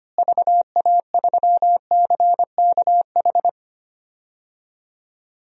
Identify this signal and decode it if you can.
Morse code, heard as a keyed tone, about 25 wpm, 700 Hz. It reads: VA3CKH
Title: DXCC-100 Call Signs - 6 Characters - Repeated 3x - 25wpm